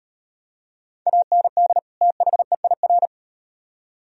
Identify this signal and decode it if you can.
Morse code, heard as a keyed tone, about 36 wpm, 700 Hz. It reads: AND THEIR